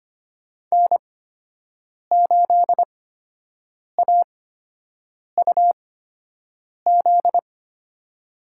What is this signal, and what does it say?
Morse code, heard as a keyed tone, about 25 wpm, spaced as for 7 wpm, 700 Hz. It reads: N8AUZ
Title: USA Call Signs - 5 Characters - 8x Character Spacing - 25wpm